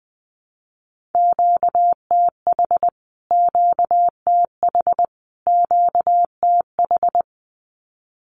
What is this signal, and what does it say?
Morse code, heard as a keyed tone, about 20 wpm, 700 Hz. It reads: QTH QTH QTH
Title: Q-Codes Repeated 3x 20wpm